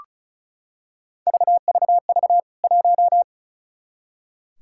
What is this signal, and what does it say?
Morse code, heard as a keyed tone, about 34 wpm, 700 Hz. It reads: VVV 1
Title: K1USN SST QSO Practice 35wpm